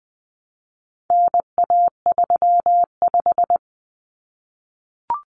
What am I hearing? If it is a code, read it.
Morse code, heard as a keyed tone, about 20 wpm, 700 Hz. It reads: NA35
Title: Common Cut Numbers - 4-Digit 20wpm